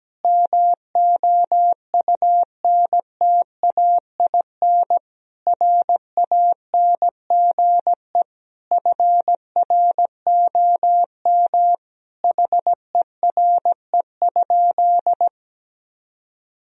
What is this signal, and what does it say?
Morse code, heard as a keyed tone, about 17 wpm, 700 Hz. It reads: MOUNTAIN RANGE FROM HERE?
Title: Sentences from Top 1000 Words - Review Part 4 - 17wpm